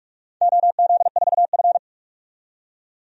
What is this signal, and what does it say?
Morse code, heard as a keyed tone, about 42 wpm, 700 Hz. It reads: OZ4F
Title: DXCC-100 Call Signs - 4 Characters - 45wpm